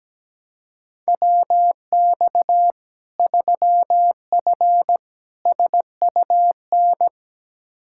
Morse code, heard as a keyed tone, about 17 wpm, 700 Hz. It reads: WX 3F SUN
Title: Practice QSO - Exchange 2 - Info - 17wpm